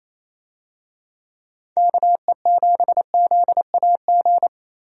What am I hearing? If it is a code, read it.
Morse code, heard as a keyed tone, about 27 wpm, 700 Hz. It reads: KE7ZAG
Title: USA Call Signs - 6 Characters - Repeated 3x - 28wpm